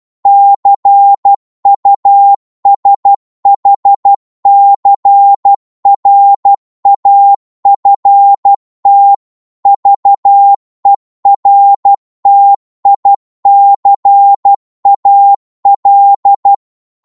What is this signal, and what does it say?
Morse code, heard as a keyed tone, about 12 wpm, 800 Hz. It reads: CUSHCRAFT VERTICAL
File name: Morse / qso-12wpm